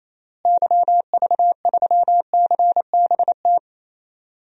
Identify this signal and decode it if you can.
Morse code, heard as a keyed tone, about 28 wpm, 700 Hz. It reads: YV3CBT
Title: DXCC-100 Call Signs - 6 Characters - Repeated 3x - 28wpm